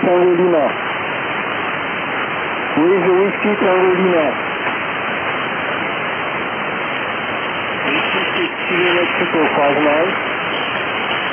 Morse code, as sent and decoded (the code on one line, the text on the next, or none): none